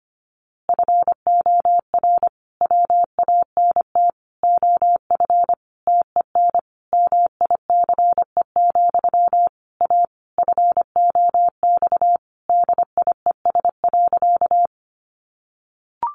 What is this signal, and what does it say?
Morse code, heard as a keyed tone, about 25 wpm, 700 Hz. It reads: FOR WANT OF TEN MICE, A FOX DIES.
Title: A Sound Of Thunder 25wpm